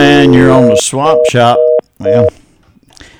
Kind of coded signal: telephone call progress tone